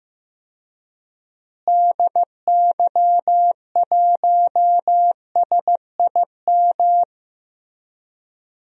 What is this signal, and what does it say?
Morse code, heard as a keyed tone, about 15 wpm, 700 Hz. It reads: DY1SIM
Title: DXCC-100 Call Signs - 6 Characters - Repeated 3x - 15wpm